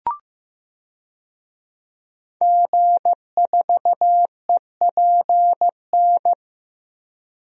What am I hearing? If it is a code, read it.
Morse code, heard as a keyed tone, about 16 wpm, 700 Hz. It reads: G4EPN
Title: DXCC-100 Call Signs - 5 Characters - Repeated 2x - 15wpm